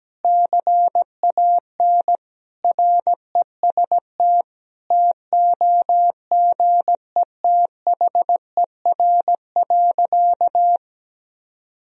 Morse code, heard as a keyed tone, about 17 wpm, 700 Hz. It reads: CAN REST TOGETHER.
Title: Sentences from Top 500 Words and Abbreviations 17wpm